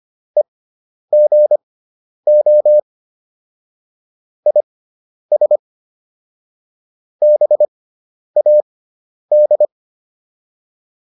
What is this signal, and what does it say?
Morse code, heard as a keyed tone, about 25 wpm, spaced as for 10 wpm, 600 Hz. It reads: EGO IS BAD